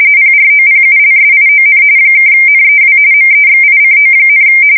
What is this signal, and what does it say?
RTTY, about 45.45 baud. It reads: PROPULSION (NEW MEXICO,
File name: fsk170 / battleships